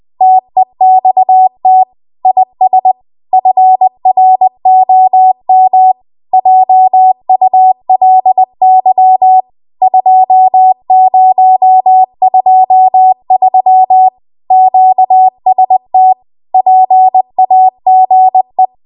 Morse code, heard as a keyed tone, about 20 wpm, 750 Hz. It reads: TEXT IS FROM JULY 2023 QST PAGE
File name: Morse / 20WPM